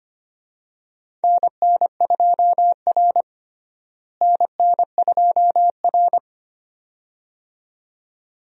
Morse code, heard as a keyed tone, about 25 wpm, 700 Hz. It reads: NN2R NN2R
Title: USA Call Signs - 4 Characters - Repeated 2x - 25wpm